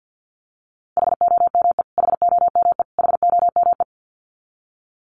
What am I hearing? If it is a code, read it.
Morse code, heard as a keyed tone, about 49 wpm, 700 Hz. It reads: HOME HOME HOME
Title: Top 200 Words Repeated 50wpm